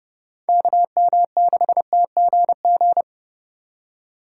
Morse code, heard as a keyed tone, about 28 wpm, 700 Hz. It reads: KM6TGG